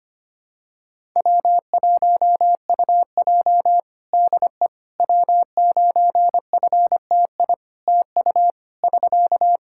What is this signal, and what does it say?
Morse code, heard as a keyed tone, about 25 wpm, 700 Hz. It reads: W1UJ DE W9FTI TU <SK>
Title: Practice QSO - Exchange 4 - Ending - 25wpm